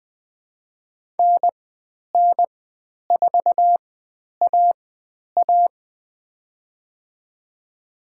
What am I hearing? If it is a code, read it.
Morse code, heard as a keyed tone, about 20 wpm, spaced as for 10 wpm, 700 Hz. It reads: NN4AA
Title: 109 Lesson 18g - Letter m - Call Signs 20wpm